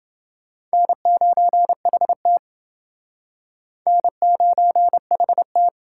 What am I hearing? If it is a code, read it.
Morse code, heard as a keyed tone, about 28 wpm, 700 Hz. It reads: N9HT N9HT